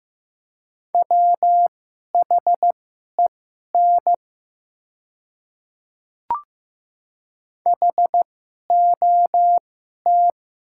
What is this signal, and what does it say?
Morse code, heard as a keyed tone, about 15 wpm, spaced as for 11 wpm, 700 Hz. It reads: WHEN HOT
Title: 2x Character Spacing - Top 100 Words - 15wpm